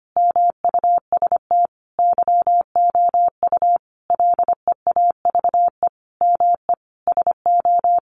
Morse code, heard as a keyed tone, about 25 wpm, 700 Hz. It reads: MUST YOU LEAVE ME SO
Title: Sentences from Top 400 Words 25wpm